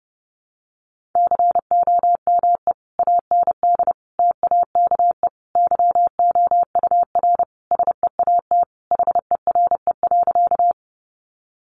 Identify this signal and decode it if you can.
Morse code, heard as a keyed tone, about 28 wpm, 700 Hz. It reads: COME AND TAKE YOUR SEAT HERE.
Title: Sentences from Top 1500 Words 30wpm